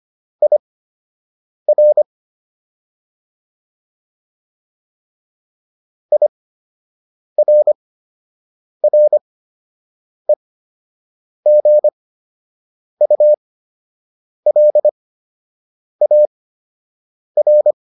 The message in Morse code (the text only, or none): IR IRREGULAR